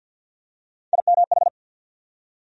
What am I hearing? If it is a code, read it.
Morse code, heard as a keyed tone, about 54 wpm, 700 Hz. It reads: IMH
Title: USA Call Sign Postfixes - 3 Letters 50wpm